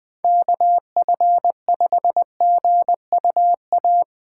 Morse code, heard as a keyed tone, about 20 wpm, 700 Hz. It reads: KF5GUA